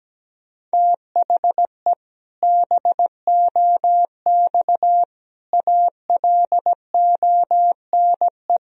Morse code, heard as a keyed tone, about 17 wpm, 700 Hz. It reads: THE BOX ALONE
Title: Sentences from Top 1000 Words - Review Part 3 - 17wpm